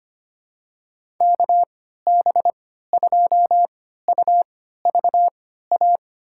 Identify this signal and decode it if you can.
Morse code, heard as a keyed tone, about 25 wpm, spaced as for 14 wpm, 700 Hz. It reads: KB2UVA
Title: USA Call Signs - 6 Characters - 3x Character Spacing - 25wpm